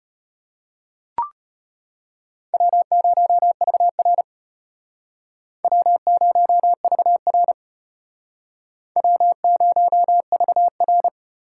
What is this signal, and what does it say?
Morse code, heard as a keyed tone, about 32 wpm, 700 Hz. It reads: W0VR W0VR W0VR